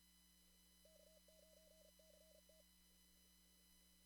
Morse code, heard as a keyed tone, about 35 wpm, 630 Hz. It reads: K9CT
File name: CW / kt5j wpx cw-op-k5pi